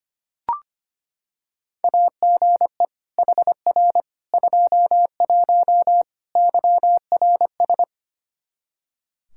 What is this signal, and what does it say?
Morse code, heard as a keyed tone, about 25 wpm, 700 Hz. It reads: AGE HR 21 YRS
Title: K1USN SST QSO Practice 25wpm